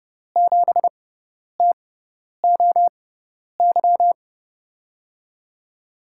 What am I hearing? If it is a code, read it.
Morse code, heard as a keyed tone, about 30 wpm, spaced as for 10 wpm, 700 Hz. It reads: 7TOY
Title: USA Call Signs - 6 Characters - 6x Character Spacing - 30wpm